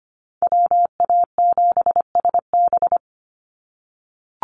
Morse code, heard as a keyed tone, about 25 wpm, 700 Hz. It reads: WA7SB
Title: NA SOTA Chaser Callsigns 25wpm